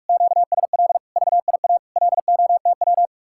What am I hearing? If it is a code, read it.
Morse code, heard as a keyed tone, about 50 wpm, 700 Hz. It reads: QSL VIA LOTW